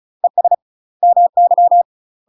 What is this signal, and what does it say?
Morse code, heard as a keyed tone, about 34 wpm, 700 Hz. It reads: ES MY